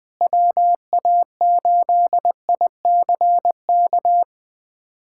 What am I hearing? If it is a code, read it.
Morse code, heard as a keyed tone, about 20 wpm, 700 Hz. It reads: WA8ICK
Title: USA Callsign DE Callsign - 6 Characters - 20wpm